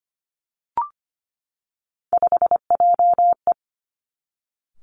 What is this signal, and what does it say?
Morse code, heard as a keyed tone, about 25 wpm, 700 Hz. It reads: 5JE
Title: Three Challenging Letter-Number Combinations 25wpm